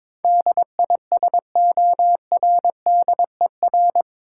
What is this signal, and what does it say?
Morse code, heard as a keyed tone, about 22 wpm, 700 Hz. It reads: DISORDER
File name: practice / 22wpm / Eight Letter Words - 22wpm